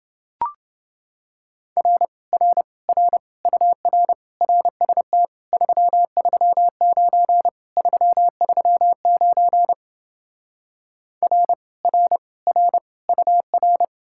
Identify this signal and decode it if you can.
Morse code, heard as a keyed tone, about 29 wpm, 700 Hz. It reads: R R R UR RST 339 339 R R R UR